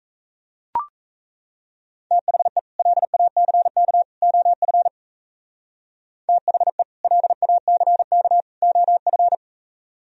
Morse code, heard as a keyed tone, about 37 wpm, 700 Hz. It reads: THE LACK OF THE LACK OF